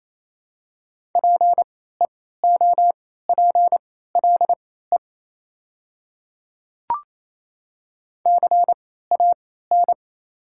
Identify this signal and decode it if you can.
Morse code, heard as a keyed tone, about 28 wpm, spaced as for 16 wpm, 700 Hz. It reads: PEOPLE CAN
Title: 3x Character Spacing - Top 100 Words - 28wpm